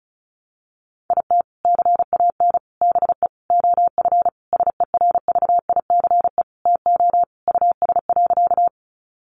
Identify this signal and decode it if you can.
Morse code, heard as a keyed tone, about 34 wpm, 700 Hz. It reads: IT CAN BE OF SERVICE TO US.
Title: Sentences from Top 1200 Words 35wpm